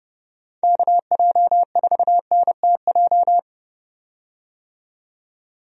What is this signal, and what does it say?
Morse code, heard as a keyed tone, about 30 wpm, 700 Hz. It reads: KJ4NTJ
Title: USA Call Signs - 6 Characters - 30wpm